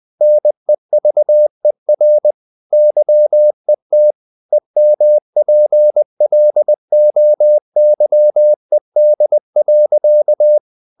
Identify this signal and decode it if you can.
Morse code, heard as a keyed tone, about 20 wpm, 600 Hz. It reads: NEVER YET EMPLOYED.
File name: the old man and the sea part 2 20wpm